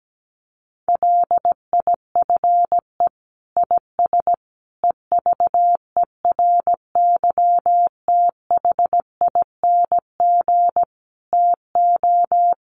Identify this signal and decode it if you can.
Morse code, heard as a keyed tone, about 17 wpm, 700 Hz. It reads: LIFE IS EVERYTHING TO
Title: Sentences from Top 700 Words 17wpm